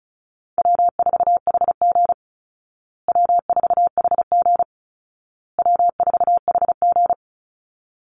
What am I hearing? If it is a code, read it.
Morse code, heard as a keyed tone, about 34 wpm, 700 Hz. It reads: W4HG W4HG W4HG
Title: Call Signs Repeated 2x - 35wpm